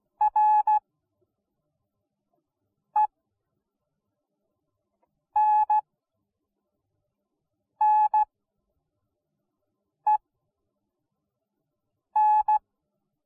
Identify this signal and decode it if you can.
Morse code, heard as a keyed tone, about 16 wpm, spaced as for 4 wpm, 850 Hz. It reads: RENNEN